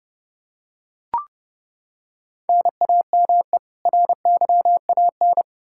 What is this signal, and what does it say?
Morse code, heard as a keyed tone, about 30 wpm, 700 Hz. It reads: NAME RYAN
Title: K1USN SST QSO Practice 30wpm